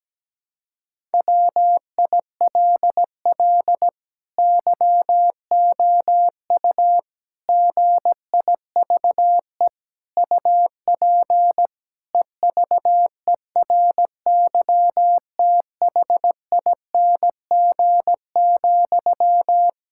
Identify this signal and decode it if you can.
Morse code, heard as a keyed tone, about 17 wpm, 700 Hz. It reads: WILL YOU GIVE UP EVERYTHING,
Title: Sentences from Top 1000 Words - Review Part 5 - 17wpm